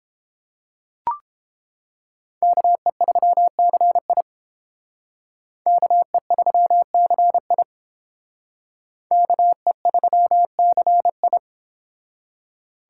Morse code, heard as a keyed tone, about 28 wpm, 700 Hz. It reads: KE3CI KE3CI KE3CI